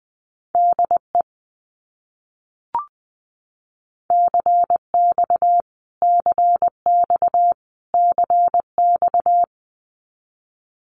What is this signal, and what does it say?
Morse code, heard as a keyed tone, about 19 wpm, 700 Hz. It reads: DE CX CX CX
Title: Abbreviations Repeated 3x - 20wpm